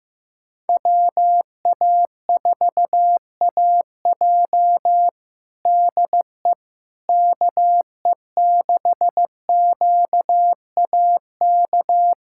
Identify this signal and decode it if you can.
Morse code, heard as a keyed tone, about 15 wpm, 700 Hz. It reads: WA4AJ DE KE6QAK
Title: USA Callsign DE Callsign 15wpm